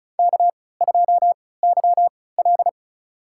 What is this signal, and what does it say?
Morse code, heard as a keyed tone, about 36 wpm, spaced as for 20 wpm, 700 Hz. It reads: K2YL
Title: USA Call Signs - 4 Characters - 3x Character Spacing - 35wpm